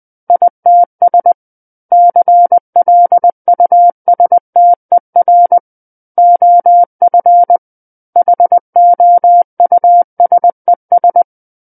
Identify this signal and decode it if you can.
Morse code, heard as a keyed tone, about 20 wpm, 700 Hz. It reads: ITS CLUSTER OF HOUSES
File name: 20 WPM